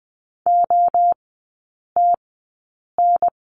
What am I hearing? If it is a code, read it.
Morse code, heard as a keyed tone, about 20 wpm, spaced as for 8 wpm, 700 Hz. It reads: OTN
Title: 074 Lesson 13e - Letter L - Three Characters 20wpm